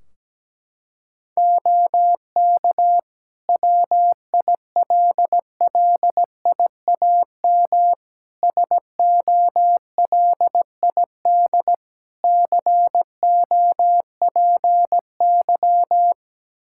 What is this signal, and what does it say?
Morse code, heard as a keyed tone, about 17 wpm, 700 Hz. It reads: OK WILLIAM SOLID COPY
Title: Practice QSO - Exchange 3 - More Info - 17wpm